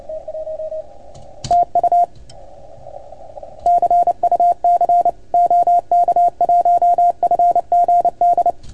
Morse code, heard as a keyed tone, about 28 wpm, 690 Hz. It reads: TU CUC OK1FGD
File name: Archiv / CUC310 / OK1FGD CUC 310 pileup